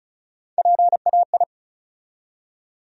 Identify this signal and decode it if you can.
Morse code, heard as a keyed tone, about 36 wpm, 700 Hz. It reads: PAI
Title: USA Call Sign Postfixes - 3 Letters 35wpm